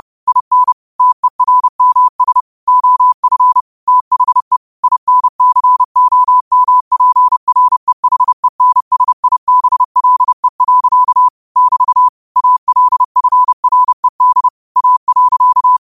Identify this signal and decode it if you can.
Morse code, heard as a keyed tone, about 30 wpm, 1000 Hz. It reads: IN TERMS OF THE INCOMPREHENSIBLE. <BT> ALFRED A.
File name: qotdfiles / 30wpm